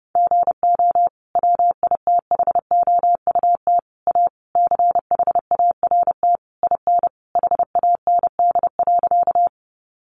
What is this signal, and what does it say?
Morse code, heard as a keyed tone, about 30 wpm, 700 Hz. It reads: GO WITHOUT A CHART IN HAND.
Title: Sentences from Top 1300 Words 30wpm